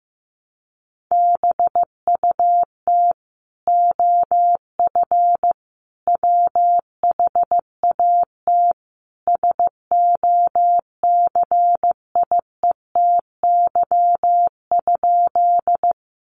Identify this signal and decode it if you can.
Morse code, heard as a keyed tone, about 15 wpm, 700 Hz. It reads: BUT OF WHAT SOCIETY?
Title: Sentences from Top 700 Words 15wpm